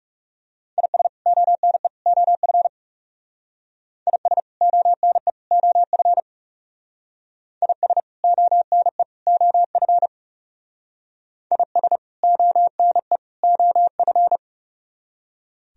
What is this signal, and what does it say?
Morse code, heard as a keyed tone, about 35 wpm, 700 Hz. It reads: IS ONE OF IS ONE OF IS ONE OF IS ONE OF
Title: Speed-Racing - Sets of 3 Words 30wpm